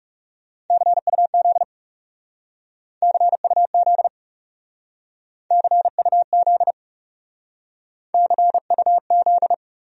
Morse code, heard as a keyed tone, about 37 wpm, 700 Hz. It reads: CUZ CUZ CUZ CUZ
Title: Speed-Racing Abbreviations - 30wpm